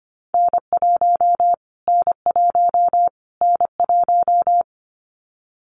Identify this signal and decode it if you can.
Morse code, heard as a keyed tone, about 25 wpm, 700 Hz. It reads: N1 N1 N1
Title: Callsign Prefixes for Top 25 DXCC - Repeated 3x - 25wpm